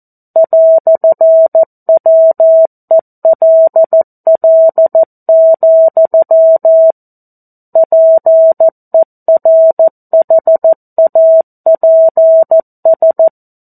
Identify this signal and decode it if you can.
Morse code, heard as a keyed tone, about 14 wpm, 650 Hz. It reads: "WELL, PERHAPS
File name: 02-Chapter-02 14 WPM